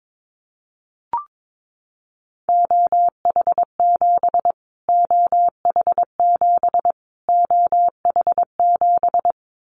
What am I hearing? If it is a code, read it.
Morse code, heard as a keyed tone, about 22 wpm, 700 Hz. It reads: OH7 OH7 OH7
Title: Callsign Prefixes for Top 25 DXCC - Repeated 3x - 22wpm